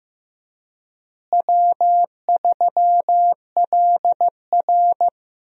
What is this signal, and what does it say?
Morse code, heard as a keyed tone, about 16 wpm, 700 Hz. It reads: W3LR
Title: USA Call Signs - 4 Characters - Repeated 3x - 15wpm